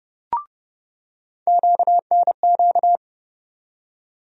Morse code, heard as a keyed tone, about 31 wpm, 700 Hz. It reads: QNQ
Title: USA Call Sign Postfixes - 3 Letters 30wpm